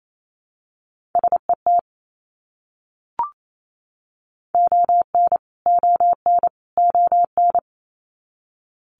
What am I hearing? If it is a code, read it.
Morse code, heard as a keyed tone, about 28 wpm, 700 Hz. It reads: SET ON ON ON
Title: Top 300 Words Repeated 28wpm